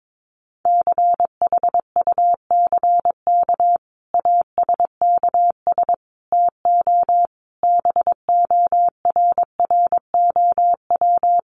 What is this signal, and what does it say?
Morse code, heard as a keyed tone, about 22 wpm, 700 Hz. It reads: CHUCK ASKS TO BORROW
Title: Additional Easy Phrases 22wpm